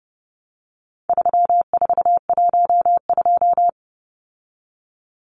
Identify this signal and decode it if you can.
Morse code, heard as a keyed tone, about 30 wpm, 700 Hz. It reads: 3412
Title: Numbers - 4-Digit 30wpm